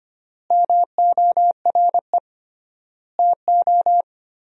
Morse code, heard as a keyed tone, about 25 wpm, 700 Hz. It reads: MORE TO
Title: 3x Word-Spacing - Sentences from Top 400 Words - 25wpm